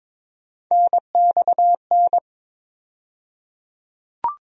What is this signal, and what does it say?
Morse code, heard as a keyed tone, about 22 wpm, 700 Hz. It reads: NXN
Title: USA Call Sign Postfixes - 3 Letters 22wpm